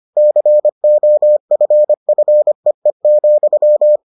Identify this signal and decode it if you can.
Morse code, heard as a keyed tone, about 25 wpm, 600 Hz. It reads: COFFEE,